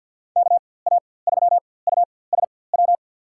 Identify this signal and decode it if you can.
Morse code, heard as a keyed tone, about 52 wpm, spaced as for 24 wpm, 700 Hz. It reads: KA3USW
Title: USA Call Signs - 6 Characters - 4x Character Spacing - 50wpm